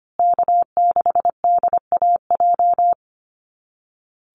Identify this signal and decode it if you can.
Morse code, heard as a keyed tone, about 25 wpm, 700 Hz. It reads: K6DAJ
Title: Call Signs Repeated 2x - 25wpm